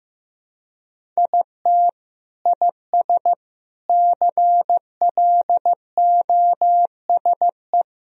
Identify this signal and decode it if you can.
Morse code, heard as a keyed tone, about 16 wpm, 700 Hz. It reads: IT IS CLOSE
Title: Sentences from Top 1000 Words - Review Part 1 - 15wpm